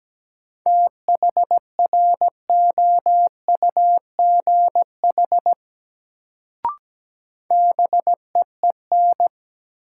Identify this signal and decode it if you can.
Morse code, heard as a keyed tone, about 17 wpm, 700 Hz. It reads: THROUGH BEEN
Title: Sets of 2 Words - New Word Drill - 17wpm